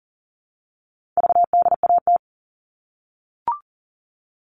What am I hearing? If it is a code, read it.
Morse code, heard as a keyed tone, about 40 wpm, 700 Hz. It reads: VDAT
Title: All Cut Numbers - 4-Digit 40wpm